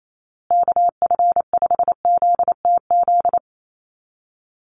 Morse code, heard as a keyed tone, about 28 wpm, 700 Hz. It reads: KF5ZTZ
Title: USA Call Signs Repeated 3x 28wpm